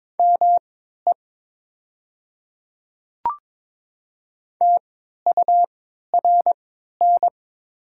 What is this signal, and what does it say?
Morse code, heard as a keyed tone, about 22 wpm, spaced as for 12 wpm, 700 Hz. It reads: ME TURN
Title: 3x Character Spacing - Top 200 Words - 22wpm